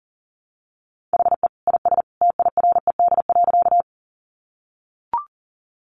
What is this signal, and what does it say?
Morse code, heard as a keyed tone, about 40 wpm, 700 Hz. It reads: HE IS TIRED.
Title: Additional Easy Phrases 40wpm